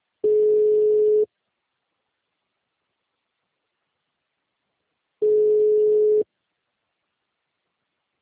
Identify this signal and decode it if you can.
Ringback tone (Europe, 425 Hz).